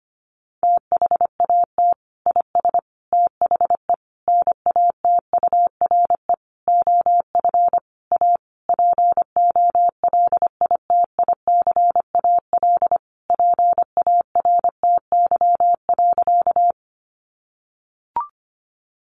Morse code, heard as a keyed tone, about 25 wpm, 700 Hz. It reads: THAT IS THE NATURE OF A POLITICAL PARTY.
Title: Sentences from Top 1400 Words 25wpm